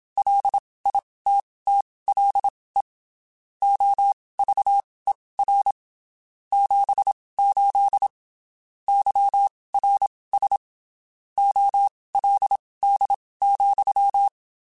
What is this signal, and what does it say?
Morse code, heard as a keyed tone, about 27 wpm, spaced as for 19 wpm, 800 Hz. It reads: LITTLE OVER 78 YRS OLD,
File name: (spaced) / QSOs Spaced-mp3s / kc7haa @25 sp